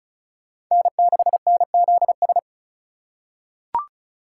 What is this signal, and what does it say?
Morse code, heard as a keyed tone, about 36 wpm, 700 Hz. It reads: N6NZS
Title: USA Call Signs - 5 Characters - 4x Character Spacing - 35wpm